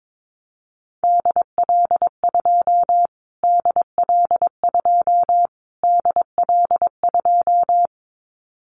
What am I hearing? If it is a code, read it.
Morse code, heard as a keyed tone, about 22 wpm, 700 Hz. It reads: DL2 DL2 DL2
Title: Callsign Prefixes for Top 25 DXCC - Repeated 3x - 22wpm